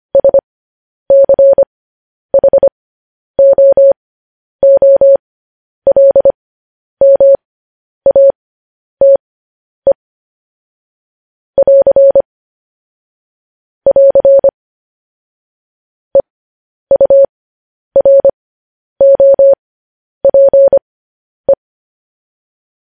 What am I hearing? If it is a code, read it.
Morse code, heard as a keyed tone, about 25 wpm, spaced as for 10 wpm, 550 Hz. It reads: SCHOOLMATE <AR> <AR> EUROPE